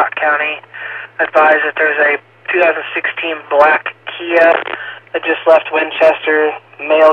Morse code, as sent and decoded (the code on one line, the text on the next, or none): none